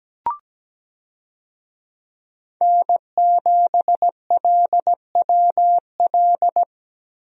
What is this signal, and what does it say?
Morse code, heard as a keyed tone, about 17 wpm, 700 Hz. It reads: N7LWL
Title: USA Call Signs - 5 Characters - Repeated 3x - 17wpm